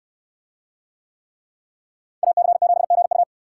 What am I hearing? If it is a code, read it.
Morse code, heard as a keyed tone, about 69 wpm, 700 Hz. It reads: AC7GU